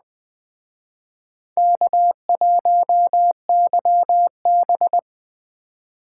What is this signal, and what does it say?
Morse code, heard as a keyed tone, about 20 wpm, 700 Hz. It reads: K1YB